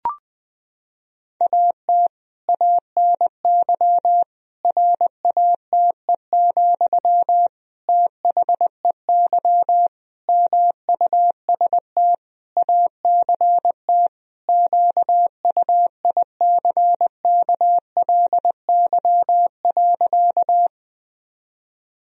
Morse code, heard as a keyed tone, about 20 wpm, 700 Hz. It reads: AT ANY RATE, THEY MUST ACT QUICKLY.
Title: Sentences from Top 1000 Words - Review Part 4 - 20wpm